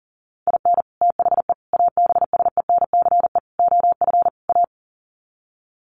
Morse code, heard as a keyed tone, about 40 wpm, 700 Hz. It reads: IN THE ABSENCE OF A
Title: Sets of 5 Words Encore 40wpm